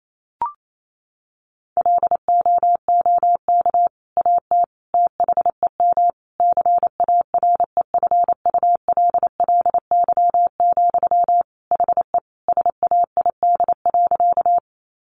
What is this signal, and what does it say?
Morse code, heard as a keyed tone, about 28 wpm, 700 Hz. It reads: LOOK AT THEM CAREFULLY, HE SAID.
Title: Sentences from Top 1300 Words 28wpm